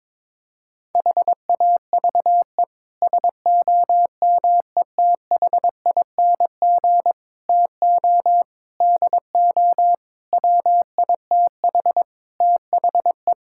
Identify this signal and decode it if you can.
Morse code, heard as a keyed tone, about 21 wpm, 700 Hz. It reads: HAVE SOMETHING TO DO WITH THE
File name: practice / 22wpm / Sets of 8 Words - 22wpm